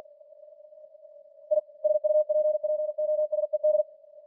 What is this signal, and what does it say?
Morse code, heard as a keyed tone, about 45 wpm, 610 Hz. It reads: I SUPPOSED